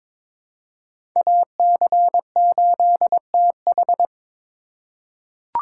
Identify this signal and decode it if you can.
Morse code, heard as a keyed tone, about 22 wpm, 700 Hz. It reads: AC8TH
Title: USA Call Signs - 5 Characters - 4x Character Spacing - 22wpm